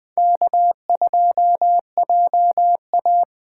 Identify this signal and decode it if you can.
Morse code, heard as a keyed tone, about 20 wpm, 700 Hz. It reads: K2JA
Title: Speed Racing - USA 1x2 Call Signs 17wpm